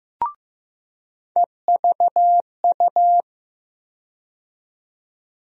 Morse code, heard as a keyed tone, about 16 wpm, 700 Hz. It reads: EVU
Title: USA Call Sign Postfixes - 3 Letters 15wpm